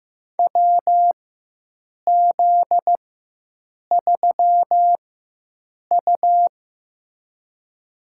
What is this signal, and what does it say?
Morse code, heard as a keyed tone, about 15 wpm, spaced as for 7 wpm, 700 Hz. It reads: WZ3U
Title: USA Call Signs - 4 Characters - 4x Character Spacing - 15wpm